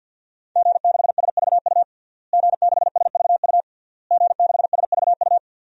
Morse code, heard as a keyed tone, about 48 wpm, 700 Hz. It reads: G6SVU G6SVU G6SVU
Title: DXCC-100 Call Signs - 5 Characters - Repeated 3x - 50wpm